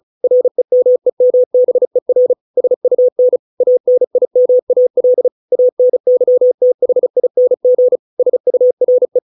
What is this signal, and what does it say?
Morse code, heard as a keyed tone, about 34 wpm, 500 Hz. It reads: REMEMBER SUN ANIMAL ANYTHING SURE